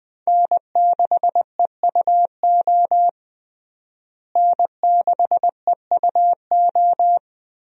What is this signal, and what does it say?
Morse code, heard as a keyed tone, about 20 wpm, 700 Hz. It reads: N6EUO N6EUO
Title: USA Call Signs - 5 Characters - Repeated 3x - 20wpm